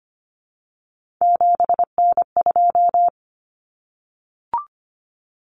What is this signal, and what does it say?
Morse code, heard as a keyed tone, about 25 wpm, 700 Hz. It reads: 7N2
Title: Common Cut Numbers - 3-Digit 25wpm